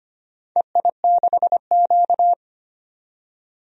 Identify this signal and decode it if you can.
Morse code, heard as a keyed tone, about 25 wpm, 700 Hz. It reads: EI6Q
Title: DXCC-100 Call Signs - 4 Characters - 25wpm